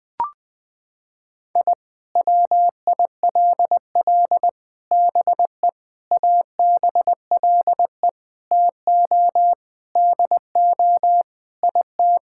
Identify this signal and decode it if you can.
Morse code, heard as a keyed tone, about 20 wpm, 700 Hz. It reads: I WILL BE ABLE TO DO IT
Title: Sentences with 4-Word Phrases - 20wpm